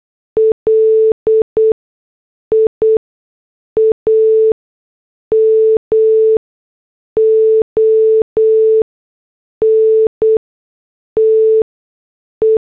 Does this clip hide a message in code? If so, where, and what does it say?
Yes: Morse code from 0.4 to 12.6 s, heard as a keyed tone, about 8 wpm, spaced as for 6 wpm, 440 Hz. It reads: LIAMONTE